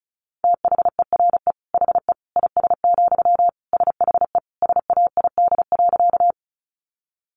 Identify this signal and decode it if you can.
Morse code, heard as a keyed tone, about 34 wpm, 700 Hz. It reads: THERE HE IS, SHE SAID.
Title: Sentences from Top 100 Words 35wpm